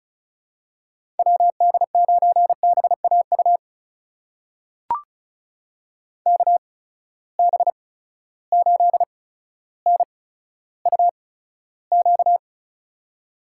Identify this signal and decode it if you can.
Morse code, heard as a keyed tone, about 36 wpm, spaced as for 15 wpm, 700 Hz. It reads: WD9BAU KB8NUQ
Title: USA Call Signs - 6 Characters - 8x Character Spacing - 35wpm